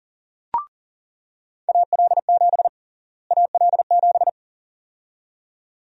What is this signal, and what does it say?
Morse code, heard as a keyed tone, about 40 wpm, 700 Hz. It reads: AL7 AL7
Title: USA Call Sign Prefixes - Repeated 2x - 40wpm